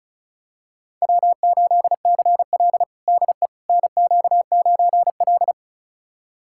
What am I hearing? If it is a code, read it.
Morse code, heard as a keyed tone, about 36 wpm, 700 Hz. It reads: W8CL DE NQ9L
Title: USA Callsign DE Callsign - 4 Characters - 35wpm